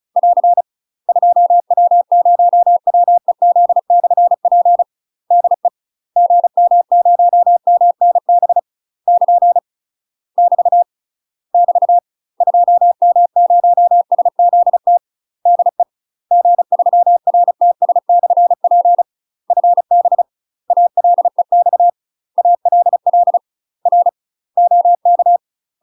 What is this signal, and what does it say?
Morse code, heard as a keyed tone, about 34 wpm, 700 Hz. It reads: <AR> 2W0WEZ/P DE GM0MNB <KN> <BT> <BT> 2M0SZT DE G3RTS/P FB ALEX ALL R OK
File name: UK MASTER QSOs TRAINING / 35WPM / UK MASTER QSOs 0040